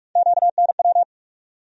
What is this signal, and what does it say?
Morse code, heard as a keyed tone, about 42 wpm, 700 Hz. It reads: QNW
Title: USA Call Sign Postfixes - 3 Letters 45wpm